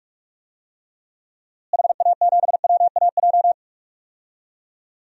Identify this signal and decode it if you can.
Morse code, heard as a keyed tone, about 45 wpm, 700 Hz. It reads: HA7WAJ
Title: DXCC-100 Call Signs - 6 Characters - Repeated 2x - 45wpm